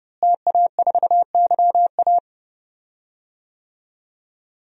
Morse code, heard as a keyed tone, about 30 wpm, 700 Hz. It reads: TA4YA
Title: DXCC-100 Call Signs - 5 Characters - Repeated 3x - 30wpm